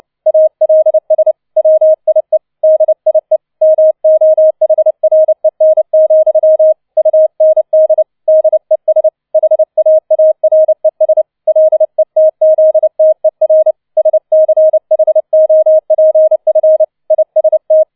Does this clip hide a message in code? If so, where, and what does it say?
Yes: Morse code from 0.3 to 17.8 s, heard as a keyed tone, about 29 wpm, 630 Hz. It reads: ALS WIE DIE MOHREN, UND DES HAARES LETZTER SCHOPF IST